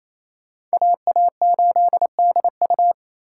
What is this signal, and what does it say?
Morse code, heard as a keyed tone, about 38 wpm, spaced as for 33 wpm, 700 Hz. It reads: AA8DU
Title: Speed Racing - USA 2x2 Call Signs 28wpm